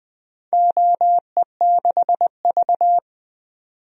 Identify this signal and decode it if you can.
Morse code, heard as a keyed tone, about 20 wpm, 700 Hz. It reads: OE6V
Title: DXCC-100 Call Signs - 4 Characters - 20wpm